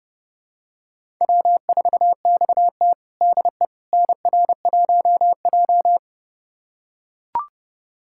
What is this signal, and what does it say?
Morse code, heard as a keyed tone, about 31 wpm, 700 Hz. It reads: W4XT DE NR1J
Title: USA Callsign DE Callsign - 4 Characters - 30wpm